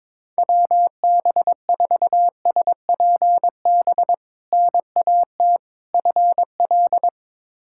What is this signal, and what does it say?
Morse code, heard as a keyed tone, about 22 wpm, 700 Hz. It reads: WB4SPB NAT FL
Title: K1USN SST QSO Practice - 22wpm